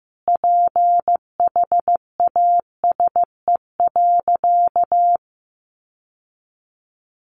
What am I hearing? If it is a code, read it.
Morse code, heard as a keyed tone, about 15 wpm, 700 Hz. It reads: PHASE.
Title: Sentences from Top 1200 Words 15wpm